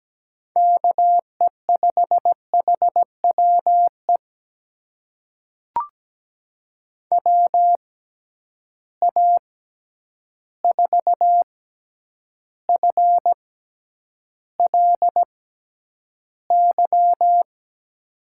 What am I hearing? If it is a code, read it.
Morse code, heard as a keyed tone, about 17 wpm, spaced as for 9 wpm, 700 Hz. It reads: KE5HWE WA4FLY
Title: USA Call Signs - 6 Characters - 6x Character Spacing - 17wpm